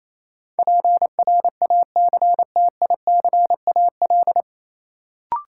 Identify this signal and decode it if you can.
Morse code, heard as a keyed tone, about 28 wpm, 700 Hz. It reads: PRACTICAL
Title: Sentences from Top 1400 Words - New Word Drill Alt - 28wpm